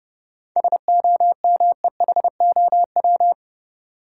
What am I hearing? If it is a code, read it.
Morse code, heard as a keyed tone, about 30 wpm, 700 Hz. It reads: SOMEHOW
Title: Seven Letter Words - 30wpm